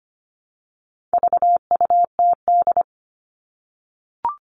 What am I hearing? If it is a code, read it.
Morse code, heard as a keyed tone, about 25 wpm, 700 Hz. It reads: VUTD
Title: All Cut Numbers - 4-Digit 25wpm